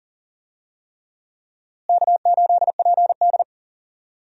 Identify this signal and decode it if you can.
Morse code, heard as a keyed tone, about 39 wpm, 700 Hz. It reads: K8PD
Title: USA Call Signs - 4 Characters - Repeated 3x - 40wpm